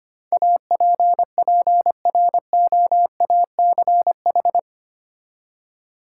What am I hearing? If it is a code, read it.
Morse code, heard as a keyed tone, about 25 wpm, 690 Hz. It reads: APPROACH
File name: 8 Letter Words - First 3 Letters Given - 25wpm